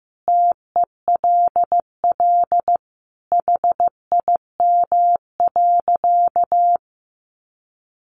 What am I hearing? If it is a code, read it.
Morse code, heard as a keyed tone, about 15 wpm, 700 Hz. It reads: TELL HIM.
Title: Sentences from Top 200 Words 15wpm